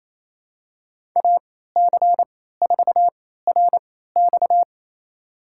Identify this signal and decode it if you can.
Morse code, heard as a keyed tone, about 28 wpm, spaced as for 16 wpm, 700 Hz. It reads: AC4RX
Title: USA Call Signs - 3x Character Spacing - 28wpm